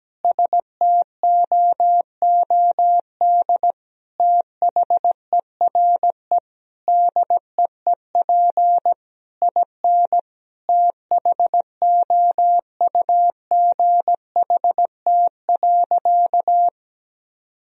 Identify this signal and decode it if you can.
Morse code, heard as a keyed tone, about 17 wpm, 700 Hz. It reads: STOOD THERE DEEP IN THOUGHT.
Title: Sentences from Top 1000 Words - Review Part 3 - 17wpm